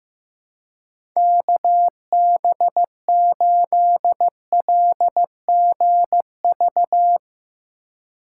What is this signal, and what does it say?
Morse code, heard as a keyed tone, about 15 wpm, 700 Hz. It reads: KB8LGV
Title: USA Call Signs - 3x Character Spacing - 15wpm